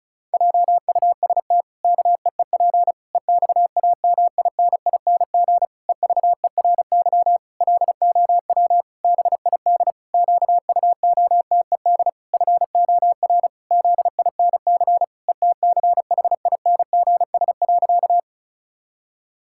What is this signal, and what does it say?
Morse code, heard as a keyed tone, about 36 wpm, 700 Hz. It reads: JUST KEEP EXAMINING EVERY LOW BID QUOTED FOR ZINC ETCHINGS.
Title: Pangram Challenge v2 - 35wpm